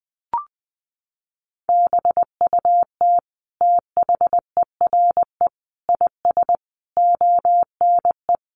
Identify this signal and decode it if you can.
Morse code, heard as a keyed tone, about 20 wpm, 700 Hz. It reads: BUT THERE IS ONE
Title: Sentences from Top 1200 Words 20wpm